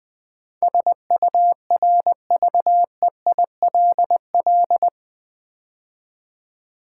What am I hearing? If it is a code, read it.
Morse code, heard as a keyed tone, about 20 wpm, 700 Hz. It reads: SURVEILL
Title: Build A Word - 4 Letters At A Time - 12-Letter Words - 20wpm